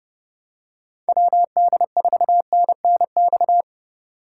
Morse code, heard as a keyed tone, about 30 wpm, 700 Hz. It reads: WD4NNX
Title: USA Call Signs - 6 Characters - 2x Character Spacing - 30wpm